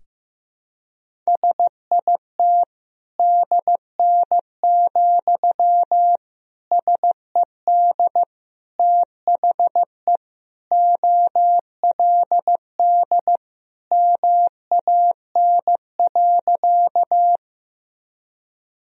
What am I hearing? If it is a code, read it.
Morse code, heard as a keyed tone, about 15 wpm, 700 Hz. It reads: SIT DN, SED THE OLD MAN.
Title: Sentences from Top 500 Words and Abbreviations 15wpm